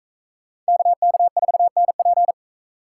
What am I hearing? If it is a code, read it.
Morse code, heard as a keyed tone, about 42 wpm, 700 Hz. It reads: KK4NP